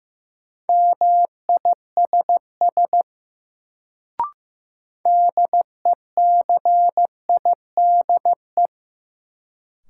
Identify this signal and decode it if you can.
Morse code, heard as a keyed tone, about 15 wpm, 700 Hz. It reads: MISS DECIDE
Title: Sentences from Top 500 Words - New Word Drill - 15wpm